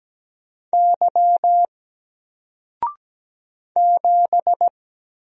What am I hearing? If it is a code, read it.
Morse code, heard as a keyed tone, about 17 wpm, 700 Hz. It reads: Y 7
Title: Single Letter-Number 17wpm